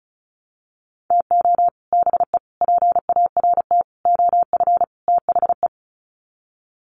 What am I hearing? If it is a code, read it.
Morse code, heard as a keyed tone, about 34 wpm, 700 Hz. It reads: TO BE PART OF THE
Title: Sets of 5 Words 35wpm